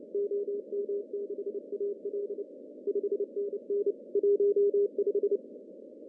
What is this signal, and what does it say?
Morse code, heard as a keyed tone, about 30 wpm, 400 Hz. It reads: OM6AL 5NN 15
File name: CW / cqww-cw-k5tr